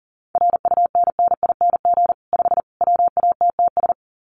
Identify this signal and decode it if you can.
Morse code, heard as a keyed tone, about 40 wpm, 700 Hz. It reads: RUNNING 5 WATTS
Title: Common QSO Phrases 40wpm